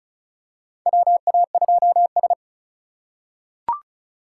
Morse code, heard as a keyed tone, about 36 wpm, 700 Hz. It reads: WA2S
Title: USA Call Signs - 4 Characters - 4x Character Spacing - 35wpm